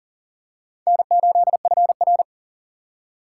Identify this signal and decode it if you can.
Morse code, heard as a keyed tone, about 40 wpm, 700 Hz. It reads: N8FR
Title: USA Call Signs - 4 Characters - 3x Character Spacing - 40wpm